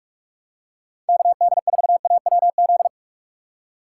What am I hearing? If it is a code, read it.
Morse code, heard as a keyed tone, about 46 wpm, 700 Hz. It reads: KD4AWZ